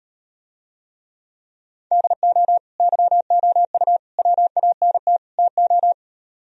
Morse code, heard as a keyed tone, about 38 wpm, 700 Hz. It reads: DO YOU WANT TO